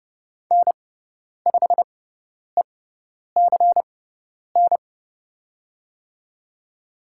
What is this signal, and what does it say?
Morse code, heard as a keyed tone, about 30 wpm, spaced as for 10 wpm, 700 Hz. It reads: N5ECN
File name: Lesson 18g - Letter m - Call Signs 30wpm